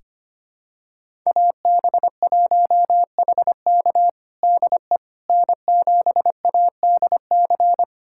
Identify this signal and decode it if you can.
Morse code, heard as a keyed tone, about 25 wpm, 700 Hz. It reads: AB1HK DE N7ADC
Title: USA Callsign DE Callsign - 5 Characters - 25wpm